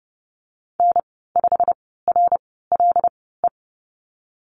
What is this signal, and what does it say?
Morse code, heard as a keyed tone, about 30 wpm, spaced as for 17 wpm, 700 Hz. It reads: N5RLE
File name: Lesson 15g - Number 5 - Call Signs 30wpm